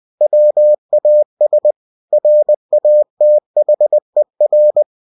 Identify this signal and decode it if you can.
Morse code, heard as a keyed tone, about 20 wpm, 600 Hz. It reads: WAS RATHER
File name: cw-books / winnie-the-pooh / winnie-20wpm-page-0091